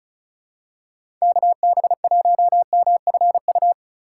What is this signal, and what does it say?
Morse code, heard as a keyed tone, about 34 wpm, 700 Hz. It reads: KB1MFU
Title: USA Call Signs - 6 Characters - 3x Character Spacing - 35wpm